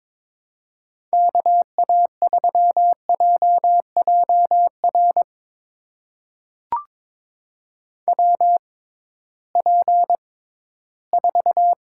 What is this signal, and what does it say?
Morse code, heard as a keyed tone, about 22 wpm, spaced as for 14 wpm, 700 Hz. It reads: KA3JJR WP4
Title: USA Call Signs - 6 Characters - 6x Character Spacing - 22wpm